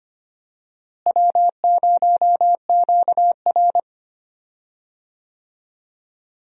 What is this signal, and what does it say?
Morse code, heard as a keyed tone, about 25 wpm, 700 Hz. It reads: W0QR
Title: USA Call Signs - 4 Characters - Repeated 2x - 25wpm